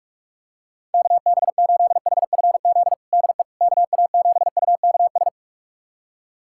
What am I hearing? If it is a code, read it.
Morse code, heard as a keyed tone, about 41 wpm, 690 Hz. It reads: KB8HFZ DE KA7UKS
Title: USA Callsign DE Callsign - 6 Characters - 45wpm